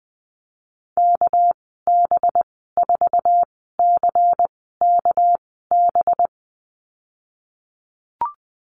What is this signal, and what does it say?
Morse code, heard as a keyed tone, about 20 wpm, spaced as for 14 wpm, 700 Hz. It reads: KB4CKB
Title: 213 Lesson 34g - Number 8 - Call Signs 20wpm